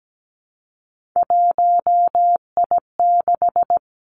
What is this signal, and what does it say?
Morse code, heard as a keyed tone, about 17 wpm, 700 Hz. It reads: 1I6
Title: Three Challenging Letter-Number Combinations 17wpm